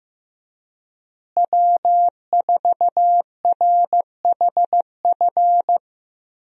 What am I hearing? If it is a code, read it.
Morse code, heard as a keyed tone, about 15 wpm, 700 Hz. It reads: W4RHF
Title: USA Call Signs - 5 Characters - Repeated 3x - 15wpm